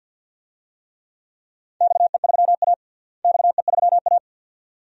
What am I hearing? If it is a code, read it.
Morse code, heard as a keyed tone, about 49 wpm, 700 Hz. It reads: XE3A XE3A
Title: DXCC-100 Call Signs - 4 Characters - Repeated 2x - 50wpm